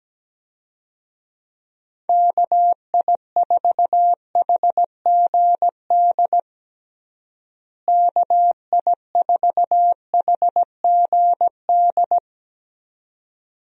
Morse code, heard as a keyed tone, about 17 wpm, 700 Hz. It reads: KI4HGD KI4HGD
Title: USA Call Signs - 6 Characters - Repeated 2x - 17wpm